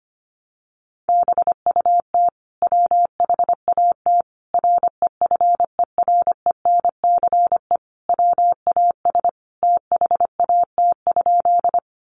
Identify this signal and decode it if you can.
Morse code, heard as a keyed tone, about 25 wpm, 700 Hz. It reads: BUT WHAT REFERENCE WAS THAT?
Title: Sentences from Top 1200 Words 25wpm